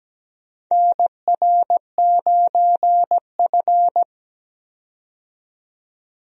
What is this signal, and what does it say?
Morse code, heard as a keyed tone, about 17 wpm, 700 Hz. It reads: NR9F
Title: USA Call Signs - 4 Characters - 17wpm